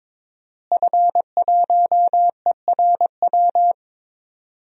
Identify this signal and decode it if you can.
Morse code, heard as a keyed tone, about 22 wpm, 700 Hz. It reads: F1ERW
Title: DXCC-100 Call Signs - 5 Characters - Repeated 2x - 22wpm